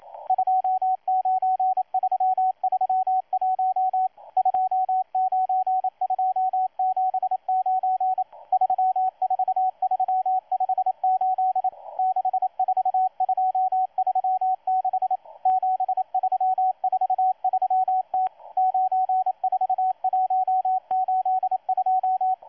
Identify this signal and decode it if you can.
Morse code, heard as a keyed tone, about 27 wpm, 750 Hz, with radio noise. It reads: 29331 29279 34358 64236 7343T 94182
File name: M12-2016-07-07-1310utc-13926khz-error-restart